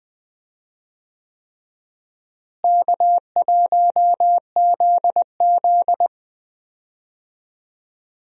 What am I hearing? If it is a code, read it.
Morse code, heard as a keyed tone, about 20 wpm, 700 Hz. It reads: K1ZZ
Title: USA Call Signs - 4 Characters - Repeated 3x - 20wpm